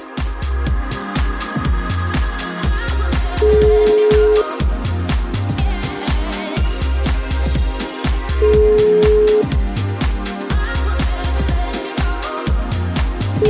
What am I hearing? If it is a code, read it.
Ringback tone (Europe, 425 Hz), with noise.